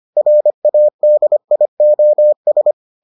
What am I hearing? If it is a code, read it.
Morse code, heard as a keyed tone, about 25 wpm, 600 Hz. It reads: RADIOS